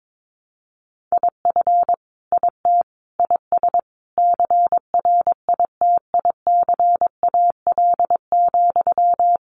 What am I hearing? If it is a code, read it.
Morse code, heard as a keyed tone, about 22 wpm, 700 Hz. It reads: IF IT IS CRITICAL,
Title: Sentences from Top 1100 Words 22wpm